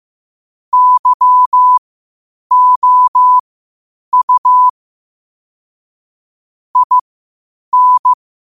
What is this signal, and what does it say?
Morse code, heard as a keyed tone, about 15 wpm, spaced as for 8 wpm, 1000 Hz. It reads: YOU IN
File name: qotdfiles / 8wpm